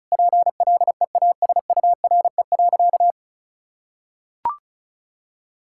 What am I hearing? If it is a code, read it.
Morse code, heard as a keyed tone, about 36 wpm, 700 Hz. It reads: PLEASURE.
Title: Sentences from Top 1600 Words 35wpm